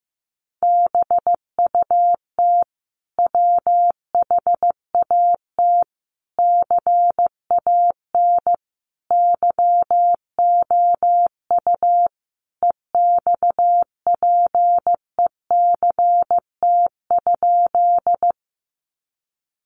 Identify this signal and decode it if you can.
Morse code, heard as a keyed tone, about 14 wpm, 700 Hz. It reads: BUT WHAT CAN YOU EXPECT?
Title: Sentences from Top 1100 Words 15wpm